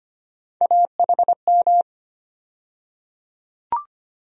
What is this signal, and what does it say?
Morse code, heard as a keyed tone, about 25 wpm, 700 Hz. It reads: AHM
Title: USA Call Sign Postfixes - 3 Letters 25wpm